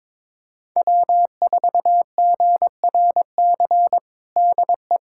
Morse code, heard as a keyed tone, about 22 wpm, 700 Hz. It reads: W4GRC DE
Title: USA Callsign DE Callsign - 5 Characters - 22wpm